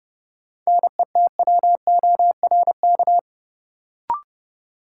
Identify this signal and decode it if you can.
Morse code, heard as a keyed tone, about 29 wpm, 700 Hz. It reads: NETWORK
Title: Sentences from Top 1000 Words - Review Part 7 - New Word Drill - 30wpm